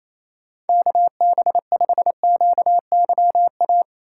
Morse code, heard as a keyed tone, about 28 wpm, 700 Hz. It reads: KB5QYA
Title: USA Call Signs - 6 Characters - 3x Character Spacing - 28wpm